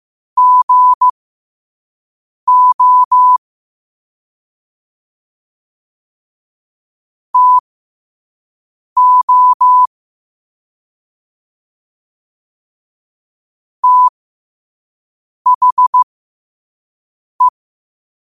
Morse code, heard as a keyed tone, about 15 wpm, spaced as for 5 wpm, 1000 Hz. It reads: GO TO THE
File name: qotdfiles / 5wpm